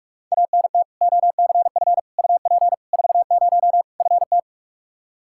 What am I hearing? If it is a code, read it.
Morse code, heard as a keyed tone, about 43 wpm, 700 Hz. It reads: ANT OCF UP 40 FT
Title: K1USN SST QSO Practice 45wpm